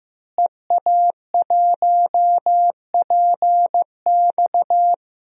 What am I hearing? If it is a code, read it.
Morse code, heard as a keyed tone, about 16 wpm, 700 Hz. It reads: EA1PX
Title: DXCC-100 Call Signs - 5 Characters - Repeated 2x - 15wpm